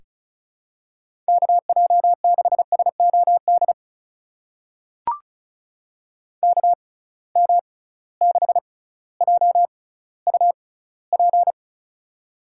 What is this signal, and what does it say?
Morse code, heard as a keyed tone, about 36 wpm, spaced as for 18 wpm, 700 Hz. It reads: KJ6SOD KM6JUP
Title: USA Call Signs - 6 Characters - 6x Character Spacing - 35wpm